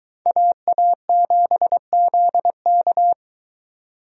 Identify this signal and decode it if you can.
Morse code, heard as a keyed tone, about 23 wpm, 700 Hz. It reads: AA7ZK